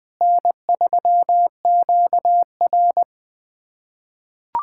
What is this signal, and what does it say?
Morse code, heard as a keyed tone, about 20 wpm, 700 Hz. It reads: N3QR
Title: USA Callsign DE Callsign - 4 Characters - 20wpm